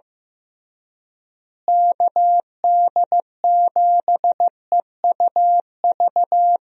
Morse code, heard as a keyed tone, about 16 wpm, 700 Hz. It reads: KD7EUV
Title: USA Call Signs - 6 Characters - Repeated 3x - 15wpm